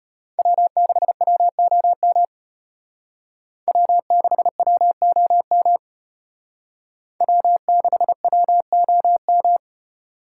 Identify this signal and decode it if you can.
Morse code, heard as a keyed tone, about 34 wpm, 700 Hz. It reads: W6WOM W6WOM W6WOM